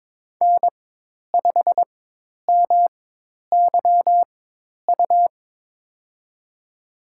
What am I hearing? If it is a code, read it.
Morse code, heard as a keyed tone, about 22 wpm, spaced as for 10 wpm, 700 Hz. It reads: N5MYU
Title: USA Call Signs - 5 Characters - 4x Character Spacing - 22wpm